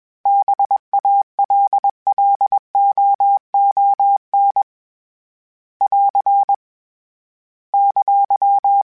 Morse code, heard as a keyed tone, about 21 wpm, 800 Hz. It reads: BALLOON <AR> !